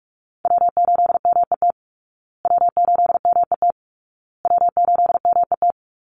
Morse code, heard as a keyed tone, about 44 wpm, 700 Hz. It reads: W8MET W8MET W8MET
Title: Call Signs Repeated 2x - 45wpm